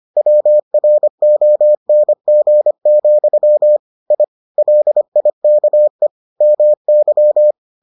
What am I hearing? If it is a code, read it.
Morse code, heard as a keyed tone, about 25 wpm, 600 Hz. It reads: WRONG, I LIKE MY